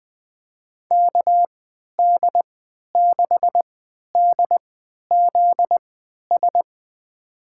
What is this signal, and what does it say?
Morse code, heard as a keyed tone, about 20 wpm, spaced as for 11 wpm, 700 Hz. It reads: KD6DZS
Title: USA Call Signs - 3x Character Spacing - 20wpm